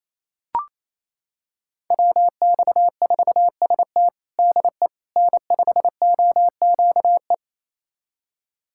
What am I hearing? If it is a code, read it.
Morse code, heard as a keyed tone, about 28 wpm, 700 Hz. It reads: WX4ST DE N5OQE
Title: USA Callsign DE Callsign 28wpm